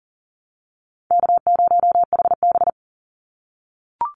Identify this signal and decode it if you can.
Morse code, heard as a keyed tone, about 40 wpm, 700 Hz. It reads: K0HB
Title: Call Signs - 40wpm